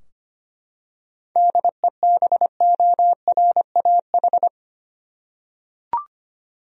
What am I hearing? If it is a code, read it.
Morse code, heard as a keyed tone, about 25 wpm, 700 Hz. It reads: DEBORAH
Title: USA Common Names - 25wpm